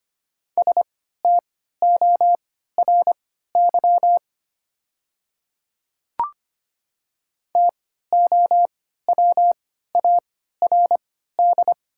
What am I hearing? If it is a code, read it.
Morse code, heard as a keyed tone, about 25 wpm, spaced as for 14 wpm, 690 Hz. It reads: STORY TOWARD
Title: 3x Character Spacing - Top 500 Words - 25wpm